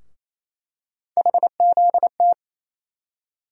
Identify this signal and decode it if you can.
Morse code, heard as a keyed tone, about 28 wpm, 700 Hz. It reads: HZT
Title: USA Call Sign Postfixes - 3 Letters 28wpm